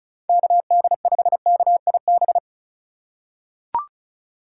Morse code, heard as a keyed tone, about 36 wpm, 700 Hz. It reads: KD5KIB
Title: USA Call Signs - 6 Characters - 3x Character Spacing - 35wpm